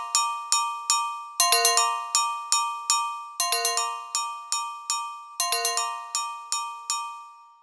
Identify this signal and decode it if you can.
There is no code.